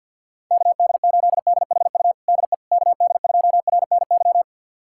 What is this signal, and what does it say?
Morse code, heard as a keyed tone, about 50 wpm, 690 Hz. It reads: KD8DHU DE KD2RNY
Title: USA Callsign DE Callsign 50wpm